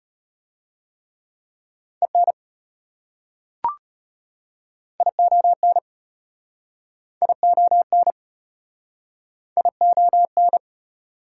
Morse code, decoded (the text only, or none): EN ION ION ION